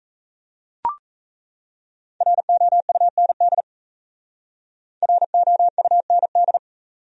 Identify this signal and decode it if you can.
Morse code, heard as a keyed tone, about 35 wpm, 700 Hz. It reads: ROUND ROUND
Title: Speed-Racing Top 500 Words 28wpm